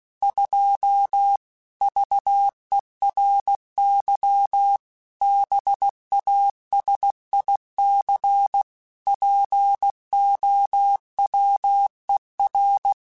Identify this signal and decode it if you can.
Morse code, heard as a keyed tone, about 16 wpm, 800 Hz. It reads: 2 VERY BASIC POWER
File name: Technical Subjects / Technical Subjects-mp3s / Sw Pwr Sup @15